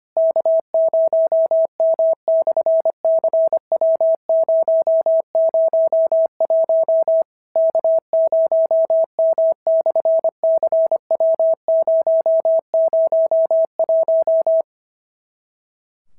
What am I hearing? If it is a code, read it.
Morse code, heard as a keyed tone, about 25 wpm, 650 Hz. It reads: K0M/CW001 K0M/CW001
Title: SOTA QSO Practice 25wpm